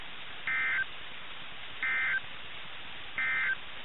Emergency Alert System (SAME) tones, heard as harsh buzzing bursts. It is heard from 0.4 to 3.6 s.
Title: NOAA Weather Radio - KIH61